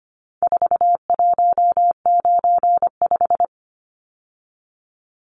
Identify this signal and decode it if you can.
Morse code, heard as a keyed tone, about 25 wpm, 700 Hz. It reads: 4195
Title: Numbers - 4-Digit 25wpm